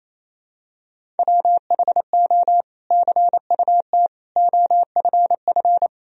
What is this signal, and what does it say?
Morse code, heard as a keyed tone, about 28 wpm, 700 Hz. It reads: WHO CUT OFF
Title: Sentences from Top 1000 Words - Review Part 1 - 28wpm